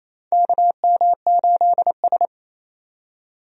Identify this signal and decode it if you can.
Morse code, heard as a keyed tone, about 27 wpm, 700 Hz. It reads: KM8S
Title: USA Call Signs - 4 Characters - 28wpm